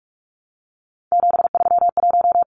Morse code, heard as a keyed tone, about 44 wpm, 700 Hz. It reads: 731
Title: Numbers - 3-Digit 45wpm